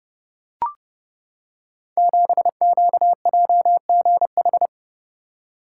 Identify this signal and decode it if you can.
Morse code, heard as a keyed tone, about 29 wpm, 700 Hz. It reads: 7QJGH
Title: Code Groups - 5 Letter:Numbers - 30wpm